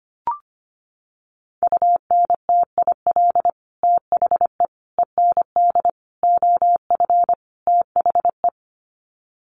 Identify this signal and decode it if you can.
Morse code, heard as a keyed tone, about 25 wpm, 700 Hz. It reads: UNTIL THE END OF THE
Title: Sets of 5 Words Encore 25wpm